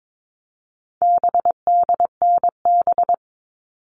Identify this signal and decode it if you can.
Morse code, heard as a keyed tone, about 22 wpm, 700 Hz. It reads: BDNB
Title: All Cut Numbers - 4-Digit 22wpm